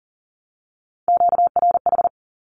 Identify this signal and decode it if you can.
Morse code, heard as a keyed tone, about 40 wpm, 700 Hz. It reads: QRH
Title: Q-Codes Repeated 3x 40wpm